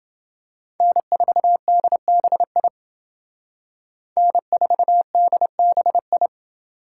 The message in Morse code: N4DBI N4DBI

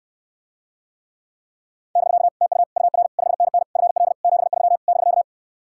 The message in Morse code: <BK> TU UR 5NN FL 73 <BK>

